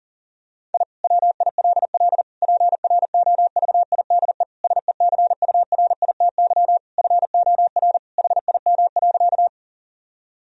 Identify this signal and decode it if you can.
Morse code, heard as a keyed tone, about 40 wpm, 700 Hz. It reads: I WILL PROVIDE SECURITY FOR HIM.
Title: Sentences from Top 1000 Words - Review Part 4 - 40wpm